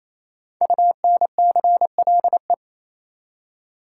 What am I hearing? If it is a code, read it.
Morse code, heard as a keyed tone, about 28 wpm, 700 Hz. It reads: UNCLE
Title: Five Letter Words - 28wpm